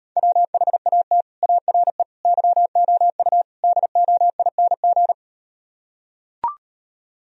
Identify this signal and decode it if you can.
Morse code, heard as a keyed tone, about 38 wpm, 700 Hz. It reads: WHAT ARE YOU DOING